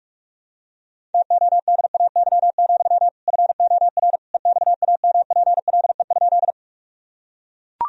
Morse code, heard as a keyed tone, about 42 wpm, 700 Hz. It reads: TODAY, FOR EXAMPLE?
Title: Sentences from Top 1000 Words - Review Part 1 - 45wpm